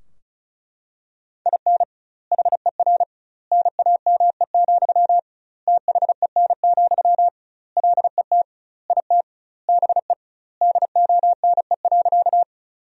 Morse code, heard as a keyed tone, about 36 wpm, 700 Hz. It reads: IN HER NAME, THEN, LET IT BE DONE.
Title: 2x Word-Spacing - Sentences from Top 500 Words - 35wpm